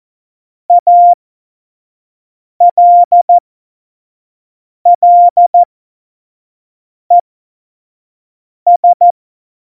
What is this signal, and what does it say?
Morse code, heard as a keyed tone, about 14 wpm, spaced as for 5 wpm, 700 Hz. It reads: ALLES
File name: Morsekurs / Witze / Witz 14. Januar 2020, 25 F70